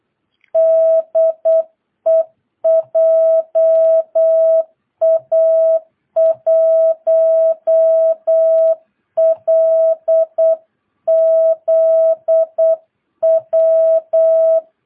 Morse code, heard as a keyed tone, about 8 wpm, 660 Hz. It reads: DE JA1LZW
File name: jh8cbh / 2022cw